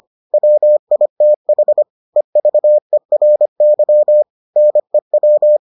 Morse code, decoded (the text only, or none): WITH EVERY NEW